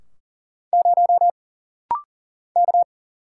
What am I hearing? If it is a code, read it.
Morse code, heard as a keyed tone, about 40 wpm, 700 Hz. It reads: 0 K
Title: Single Letter-Number Alt - Rapid-Fire 40wpm